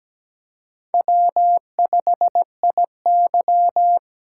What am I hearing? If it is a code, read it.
Morse code, heard as a keyed tone, about 17 wpm, 700 Hz. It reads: W5IY
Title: USA Call Signs - 4 Characters - 4x Character Spacing - 17wpm